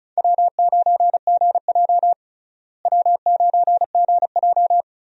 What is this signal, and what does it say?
Morse code, heard as a keyed tone, about 36 wpm, 700 Hz. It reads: W9GJ W9GJ
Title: USA Call Signs - 4 Characters - Repeated 3x - 35wpm